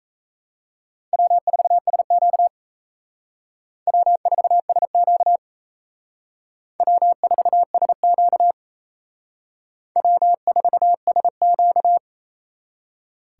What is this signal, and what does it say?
Morse code, heard as a keyed tone, about 36 wpm, 700 Hz. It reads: W4SQ W4SQ W4SQ W4SQ